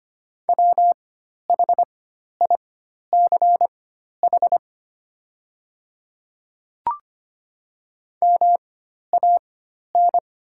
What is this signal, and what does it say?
Morse code, heard as a keyed tone, about 25 wpm, spaced as for 12 wpm, 700 Hz. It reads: WHICH MAN
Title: 4x Character Spacing - Top 500 Words - 25wpm